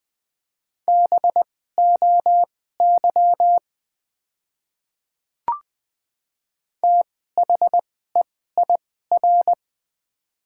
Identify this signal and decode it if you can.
Morse code, heard as a keyed tone, about 20 wpm, spaced as for 15 wpm, 700 Hz. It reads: BOY THEIR
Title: 2x Character Spacing - Top 200 Words - 20wpm